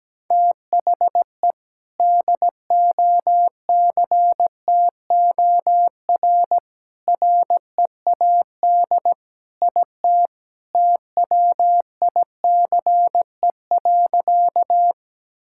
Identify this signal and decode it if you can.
Morse code, heard as a keyed tone, about 17 wpm, 700 Hz. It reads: THE DOCTOR READ IT TWICE.
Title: Sentences from Top 1600 Words 17wpm